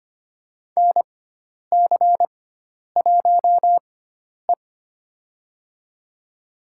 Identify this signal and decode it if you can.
Morse code, heard as a keyed tone, about 25 wpm, spaced as for 10 wpm, 700 Hz. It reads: NC1E
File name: Lesson 16g - Letter c - Call Signs 25wpm